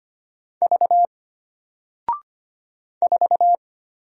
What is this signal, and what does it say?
Morse code, heard as a keyed tone, about 25 wpm, 700 Hz. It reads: V 4
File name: Single-Letter-Number-25wpm